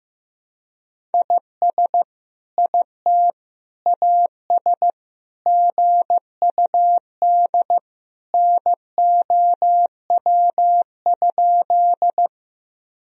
Morse code, heard as a keyed tone, about 16 wpm, 700 Hz. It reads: IS IT AS GUD NOW?
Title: Sentences from Top 200 Words and Abbreviations 15wpm